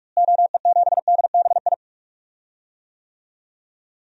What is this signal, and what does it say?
Morse code, heard as a keyed tone, about 41 wpm, 700 Hz. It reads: OE7DBI
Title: DXCC-100 Call Signs - 6 Characters - Repeated 3x - 45wpm